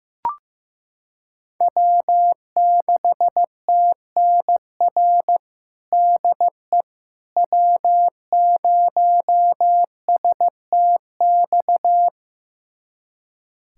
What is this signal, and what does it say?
Morse code, heard as a keyed tone, about 15 wpm, 700 Hz. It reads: W6TNR DE W0STX
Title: USA Callsign DE Callsign - 5 Characters - 15wpm